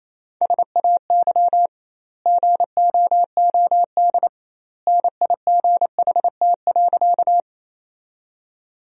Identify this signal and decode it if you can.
Morse code, heard as a keyed tone, about 28 wpm, 700 Hz. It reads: SAY GOOD NIGHT.
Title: 2x Word-Spacing - Sentences from Top 300 Words - 28wpm